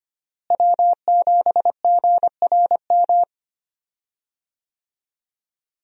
Morse code, heard as a keyed tone, about 25 wpm, 700 Hz. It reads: W7GRM
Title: USA Call Signs - 5 Characters - Repeated 2x - 25wpm